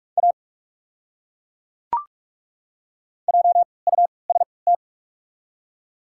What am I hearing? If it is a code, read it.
Morse code, heard as a keyed tone, about 41 wpm, spaced as for 24 wpm, 700 Hz. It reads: A JUST
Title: 3x Character Spacing - Top 300 Words - 45wpm